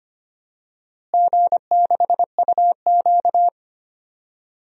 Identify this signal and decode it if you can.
Morse code, heard as a keyed tone, about 25 wpm, 700 Hz. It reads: G6UQ
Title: DXCC-100 Call Signs - 4 Characters - Repeated 3x - 25wpm